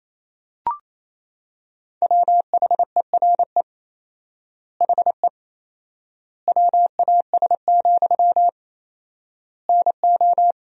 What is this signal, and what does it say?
Morse code, heard as a keyed tone, about 28 wpm, 700 Hz. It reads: WHERE HE WAS, NO
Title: 4x Word-Spacing - Sentences from Top 400 Words - 28wpm